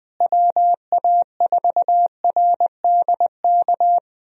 Morse code, heard as a keyed tone, about 20 wpm, 700 Hz. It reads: WA4RDK
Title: USA Call Signs - 6 Characters - Repeated 3x - 20wpm